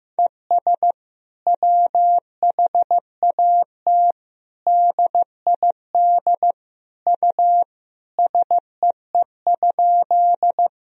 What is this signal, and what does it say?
Morse code, heard as a keyed tone, about 16 wpm, 700 Hz. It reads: ES WHAT DID U SEE?
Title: Sentences from Top 100 Words and Abbreviations 15wpm